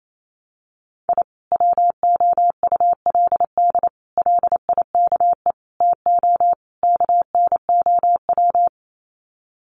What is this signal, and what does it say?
Morse code, heard as a keyed tone, about 28 wpm, 700 Hz. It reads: I WOULD LIKE TO KNOW
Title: Sets of 5 Words Encore 28wpm